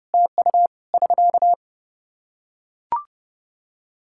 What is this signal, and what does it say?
Morse code, heard as a keyed tone, about 31 wpm, 700 Hz. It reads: TU <SK>
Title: Practice QSO - Exchange 4 - Ending - 30wpm